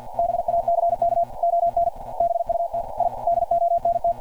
Morse code, heard as a keyed tone, about 50 wpm, 690 Hz. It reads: BACK ON BUT I DONT
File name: archived / CRSnet